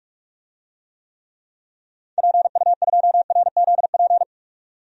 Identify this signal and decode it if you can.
Morse code, heard as a keyed tone, about 46 wpm, 700 Hz. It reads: PU2RZP
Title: DXCC-100 Call Signs - 6 Characters - 45wpm